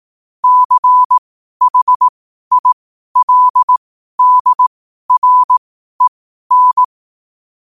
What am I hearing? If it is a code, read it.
Morse code, heard as a keyed tone, about 18 wpm, spaced as for 12 wpm, 1000 Hz. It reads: CHILDREN
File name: qotdfiles / 12wpm